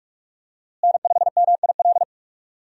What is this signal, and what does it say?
Morse code, heard as a keyed tone, about 45 wpm, 700 Hz. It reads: N5MIL
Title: USA Call Signs - 5 Characters - Repeated 3x - 45wpm